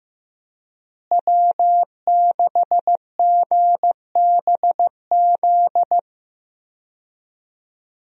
Morse code, heard as a keyed tone, about 15 wpm, 700 Hz. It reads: W6GBZ
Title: USA Call Signs - 5 Characters - Repeated 2x - 15wpm